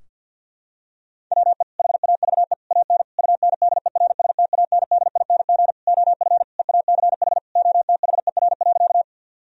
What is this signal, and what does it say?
Morse code, heard as a keyed tone, about 52 wpm, 700 Hz. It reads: WE HAVE AN UNDERSTANDING OF EACH OTHER.
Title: Sentences from Top 1000 Words - Review Part 3 - 50wpm